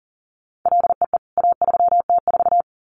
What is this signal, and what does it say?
Morse code, heard as a keyed tone, about 40 wpm, 700 Hz. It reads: LEE A3T4
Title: CWT Exchange 40wpm